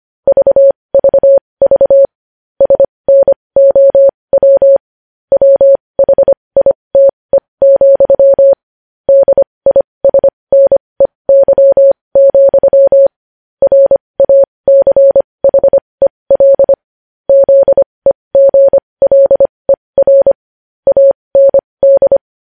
Morse code, heard as a keyed tone, about 25 wpm, spaced as for 20 wpm, 550 Hz. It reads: VVV SNOW WHITE, DISNEY, RACHEL ZEGLER AND
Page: news headlines in morse code at 20 wpm